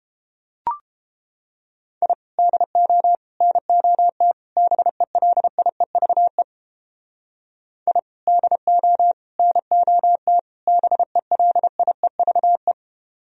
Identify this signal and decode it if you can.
Morse code, heard as a keyed tone, about 29 wpm, 700 Hz. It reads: I DO NOT BELIEVE I DO NOT BELIEVE